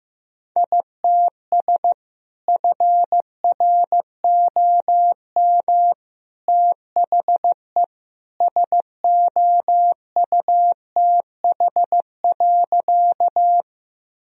Morse code, heard as a keyed tone, about 16 wpm, 700 Hz. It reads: ITS FROM THE SOUTH.
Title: Sentences from Top 1000 Words - Review Part 2 - 15wpm